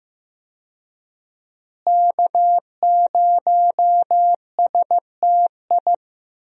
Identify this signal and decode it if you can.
Morse code, heard as a keyed tone, about 15 wpm, 700 Hz. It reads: K0STI
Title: USA Call Signs - 5 Characters - Repeated 2x - 15wpm